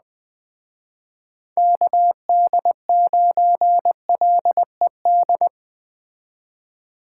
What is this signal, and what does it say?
Morse code, heard as a keyed tone, about 20 wpm, 700 Hz. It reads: KD9LED
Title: Speed Racing - USA Call Signs 17wpm